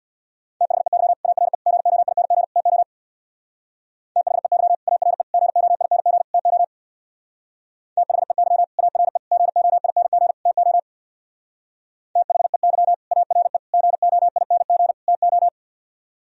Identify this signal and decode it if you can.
Morse code, heard as a keyed tone, about 62 wpm, 690 Hz. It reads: THEY ARE GOING TO THEY ARE GOING TO THEY ARE GOING TO THEY ARE GOING TO
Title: Speed-Racing - Sets of 4 Words 50wpm